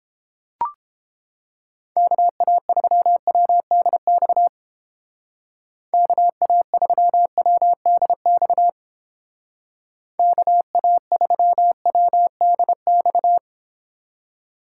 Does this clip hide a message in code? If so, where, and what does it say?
Yes: Morse code from 2.0 to 13.4 s, heard as a keyed tone, about 30 wpm, 700 Hz. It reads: KA3WDX KA3WDX KA3WDX